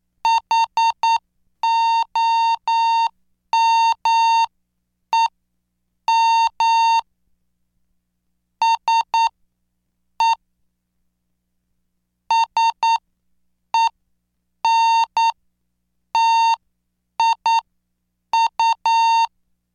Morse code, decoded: HOMEM SE SENTIU